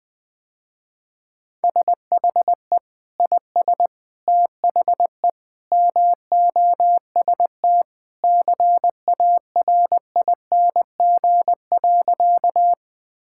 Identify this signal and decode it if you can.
Morse code, heard as a keyed tone, about 20 wpm, 700 Hz. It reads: SHE IS THE MOST CARING.
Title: Sentences with 2-Word Phrases - Spoken Beginning - 20wpm